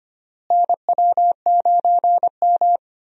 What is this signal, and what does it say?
Morse code, heard as a keyed tone, about 25 wpm, 700 Hz. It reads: NW9M
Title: USA Call Signs - 4 Characters - Repeated 3x - 25wpm